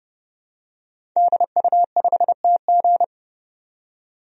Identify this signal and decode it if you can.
Morse code, heard as a keyed tone, about 30 wpm, 700 Hz. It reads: DU5TG
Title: DXCC-100 Call Signs - 5 Characters - 30wpm